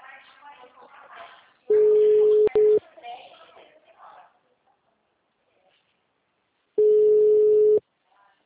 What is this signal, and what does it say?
Ringback tone (Europe, 425 Hz), with speech.
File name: out- -2002- - - .5185